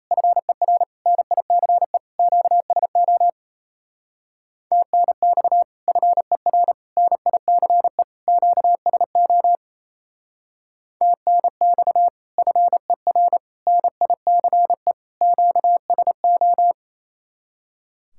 Morse code, heard as a keyed tone, about 34 wpm, 700 Hz. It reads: FER NICE QSO TNX FER NICE QSO TNX FER NICE QSO
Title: Speed-Racing Common QSO Phrases - 28wpm